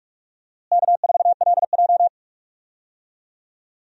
Morse code, heard as a keyed tone, about 46 wpm, 700 Hz. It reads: K4LJ
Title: USA Call Signs - 4 Characters - Repeated 3x - 45wpm